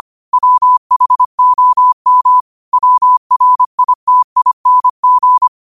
Morse code, heard as a keyed tone, about 25 wpm, 1000 Hz. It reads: WHOM WRITING